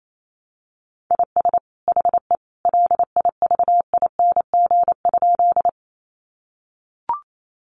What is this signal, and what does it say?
Morse code, heard as a keyed tone, about 28 wpm, 700 Hz. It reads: IS HE LIVING?
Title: Sentences from Top 1100 Words 28wpm